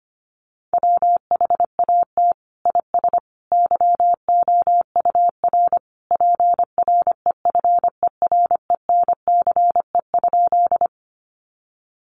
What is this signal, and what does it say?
Morse code, heard as a keyed tone, about 25 wpm, 700 Hz. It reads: WHAT IS YOUR PREFERENCE?
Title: Sentences from Top 1500 Words 25wpm